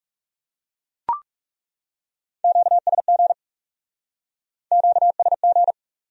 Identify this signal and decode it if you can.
Morse code, heard as a keyed tone, about 37 wpm, 700 Hz. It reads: QSG QSG
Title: Speed-Racing Q-Codes - 30wpm